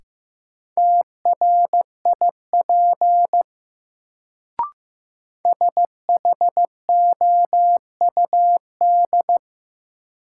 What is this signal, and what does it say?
Morse code, heard as a keyed tone, about 15 wpm, 700 Hz. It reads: TRIP SHOUD
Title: Sentences from Top 1000 Words - Review Part 4 - New Word Drill - 15wpm